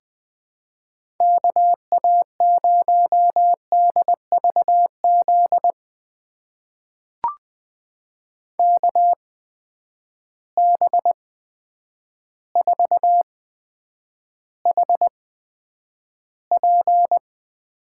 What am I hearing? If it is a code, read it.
Morse code, heard as a keyed tone, about 20 wpm, spaced as for 9 wpm, 700 Hz. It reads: KA0DVZ KB4HP
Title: USA Call Signs - 6 Characters - 8x Character Spacing - 20wpm